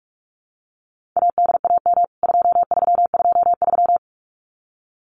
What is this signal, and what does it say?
Morse code, heard as a keyed tone, about 44 wpm, 700 Hz. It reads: ADAM 2323
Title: CWT Exchange 45wpm